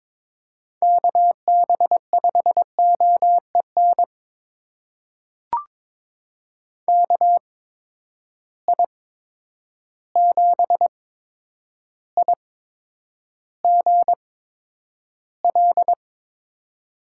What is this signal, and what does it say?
Morse code, heard as a keyed tone, about 22 wpm, spaced as for 9 wpm, 700 Hz. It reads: KB5OEN KI7IGL
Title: USA Call Signs - 6 Characters - 8x Character Spacing - 22wpm